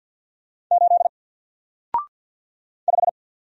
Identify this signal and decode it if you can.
Morse code, heard as a keyed tone, about 54 wpm, 700 Hz. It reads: 8 5
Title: Single Character (Letter:Number:Punctuation:Prosign) 50wpm